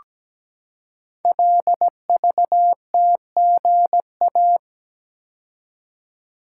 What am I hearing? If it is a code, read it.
Morse code, heard as a keyed tone, about 17 wpm, 700 Hz. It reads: LVTGA
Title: Code Groups - 5 Letters - 17wpm